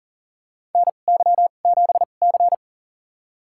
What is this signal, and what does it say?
Morse code, heard as a keyed tone, about 40 wpm, spaced as for 29 wpm, 700 Hz. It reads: NY7C
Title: USA Call Signs - 4 Characters - 2x Character Spacing - 40wpm